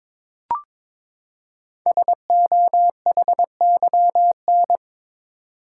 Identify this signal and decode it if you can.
Morse code, heard as a keyed tone, about 22 wpm, 700 Hz. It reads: SOHYN
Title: Code Groups - 5 Letters - 22wpm